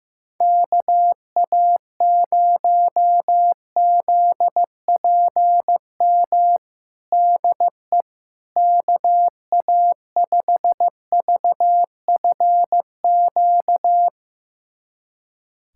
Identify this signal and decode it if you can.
Morse code, heard as a keyed tone, about 16 wpm, 700 Hz. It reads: KA0ZPM DE KA5VFQ
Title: USA Callsign DE Callsign - 6 Characters - 15wpm